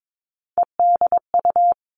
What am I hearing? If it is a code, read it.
Morse code, heard as a keyed tone, about 22 wpm, 700 Hz. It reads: EDU
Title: All Cut Numbers - 3-Digit 22wpm